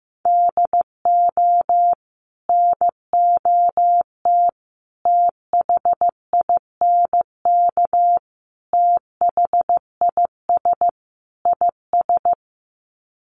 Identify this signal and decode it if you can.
Morse code, heard as a keyed tone, about 15 wpm, 700 Hz. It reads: DO NOT THINK THIS IS
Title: Sets of 5 Words Encore2 15wpm